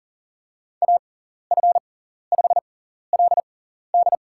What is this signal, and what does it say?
Morse code, heard as a keyed tone, about 38 wpm, spaced as for 14 wpm, 700 Hz. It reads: AF5LD
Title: USA Call Signs - 5 Characters - 6x Character Spacing - 40wpm